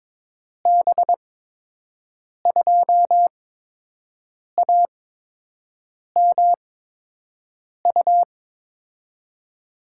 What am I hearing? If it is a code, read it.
Morse code, heard as a keyed tone, about 22 wpm, spaced as for 6 wpm, 700 Hz. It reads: B2AMU
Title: USA Call Signs - 6 Characters - 8x Character Spacing - 22wpm